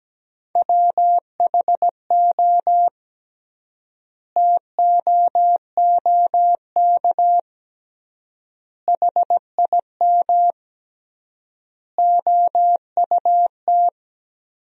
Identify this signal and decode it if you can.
Morse code, heard as a keyed tone, about 17 wpm, 700 Hz. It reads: WHO TOOK HIM OUT
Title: 3x Word-Spacing - Sentences from Top 300 Words - 17wpm